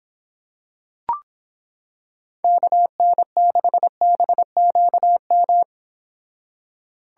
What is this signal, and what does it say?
Morse code, heard as a keyed tone, about 26 wpm, 700 Hz. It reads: KN6BQM